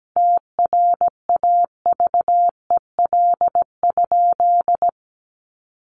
Morse code, heard as a keyed tone, about 17 wpm, 700 Hz. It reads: TRAVEL?
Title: Sentences from Top 1400 Words 17wpm